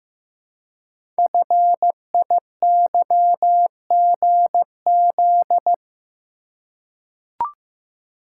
Morse code, heard as a keyed tone, about 15 wpm, 700 Hz. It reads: FIYGZ
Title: Code Groups - 5 Letters - 15wpm